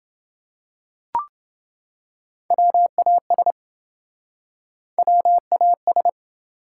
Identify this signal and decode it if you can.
Morse code, heard as a keyed tone, about 28 wpm, 700 Hz. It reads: WAS WAS